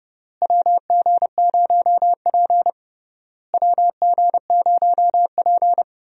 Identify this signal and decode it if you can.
Morse code, heard as a keyed tone, about 30 wpm, 700 Hz. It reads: WG0P WG0P
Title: USA Call Signs - 4 Characters - Repeated 3x - 30wpm